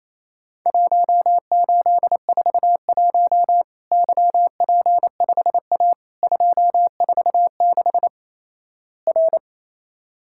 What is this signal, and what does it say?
Morse code, heard as a keyed tone, about 28 wpm, 700 Hz. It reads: 1841 YP5A 246 R
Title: Worked All Europe DX Contest - QTC Exchange - 28wpm